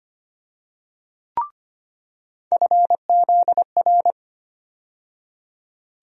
Morse code, heard as a keyed tone, about 25 wpm, 700 Hz. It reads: FZR
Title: USA Call Sign Postfixes - 3 Letters 25wpm